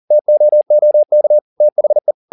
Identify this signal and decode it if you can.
Morse code, heard as a keyed tone, about 40 wpm, 600 Hz. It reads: TOOK THE